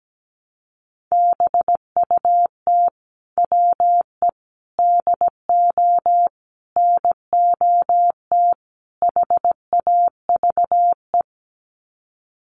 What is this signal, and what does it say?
Morse code, heard as a keyed tone, about 17 wpm, 700 Hz. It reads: BUT WE DO NOT HAVE
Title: Sets of 5 Words Encore2 17wpm